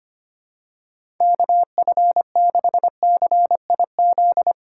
Morse code, heard as a keyed tone, about 25 wpm, 700 Hz. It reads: KF6CIZ
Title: USA Call Signs - 6 Characters - 2x Character Spacing - 25wpm